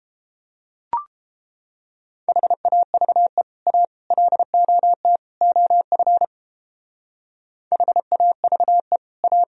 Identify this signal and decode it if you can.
Morse code, heard as a keyed tone, about 31 wpm, 700 Hz. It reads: HAVE A LOT OF HAVE A